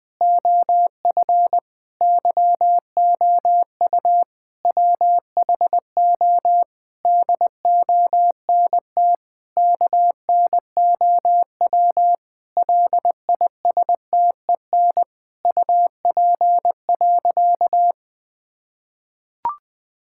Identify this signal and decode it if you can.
Morse code, heard as a keyed tone, about 20 wpm, 700 Hz. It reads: OF YOU WHO DONT KNOW LISTEN UP.
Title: Sentences with 7-Word Phrases - 20wpm